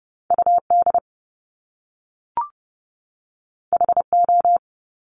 Morse code, heard as a keyed tone, about 30 wpm, 700 Hz. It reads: UD HO
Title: Top Two Letter Combinations in Words 30wpm